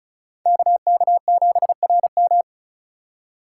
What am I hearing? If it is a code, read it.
Morse code, heard as a keyed tone, about 36 wpm, 700 Hz. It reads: KK7RM
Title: USA Call Signs - 5 Characters - 2x Character Spacing - 35wpm